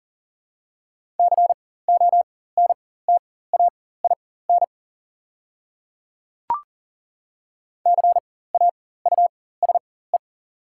Morse code, heard as a keyed tone, about 40 wpm, spaced as for 19 wpm, 690 Hz. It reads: CONTAIN CAUSE
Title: 4x Character Spacing - Top 500 Words - 40wpm